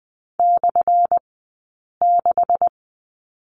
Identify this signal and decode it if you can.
Morse code, heard as a keyed tone, about 19 wpm, spaced as for 8 wpm, 700 Hz. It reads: /6
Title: 183 Lesson 30d - Letter b - Two Characters 20wpm